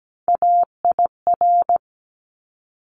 Morse code, heard as a keyed tone, about 17 wpm, 700 Hz. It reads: AIR
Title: Top 200 Words 17wpm